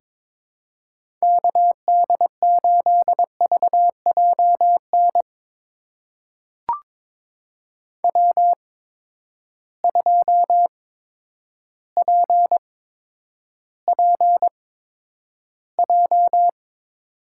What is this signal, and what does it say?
Morse code, heard as a keyed tone, about 21 wpm, spaced as for 8 wpm, 700 Hz. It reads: KD8VJ NW2PPJ
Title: USA Call Signs - 8x Character Spacing - 22wpm